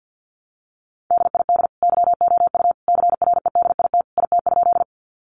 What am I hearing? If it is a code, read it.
Morse code, heard as a keyed tone, about 50 wpm, 700 Hz. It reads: DID YOU CREDIT IT?
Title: Sentences from Top 1000 Words 50wpm